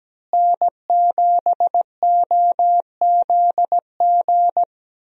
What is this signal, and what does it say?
Morse code, heard as a keyed tone, about 17 wpm, 700 Hz. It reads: N7OZG
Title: USA Call Signs - 5 Characters - Repeated 3x - 17wpm